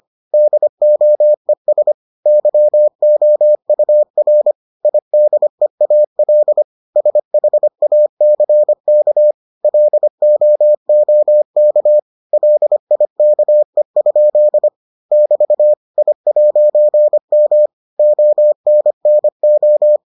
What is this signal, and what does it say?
Morse code, heard as a keyed tone, about 25 wpm, 600 Hz. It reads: DOES YOUR IDEAL SHACK LOOK LIKE? <BT> I'M ONNO